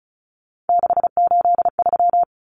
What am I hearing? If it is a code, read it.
Morse code, heard as a keyed tone, about 34 wpm, 700 Hz. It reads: 683
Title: Numbers - 3-Digit 35wpm